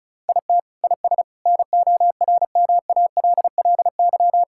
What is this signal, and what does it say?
Morse code, heard as a keyed tone, about 34 wpm, 700 Hz. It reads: IT IS NORMALLY
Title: Sentences from Top 1000 Words - Review Part 6 - 35wpm